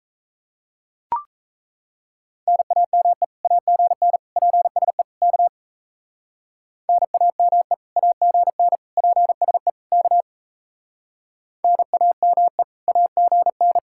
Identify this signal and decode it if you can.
Morse code, heard as a keyed tone, about 33 wpm, 700 Hz. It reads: NAME AGN PSE K NAME AGN PSE K NAME AGN
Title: Speed-Racing Common QSO Phrases - 28wpm